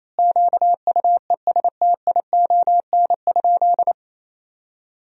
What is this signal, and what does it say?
Morse code, heard as a keyed tone, about 28 wpm, 700 Hz. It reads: QUESTION?
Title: 3x Word-Spacing - Sentences from Top 400 Words - 28wpm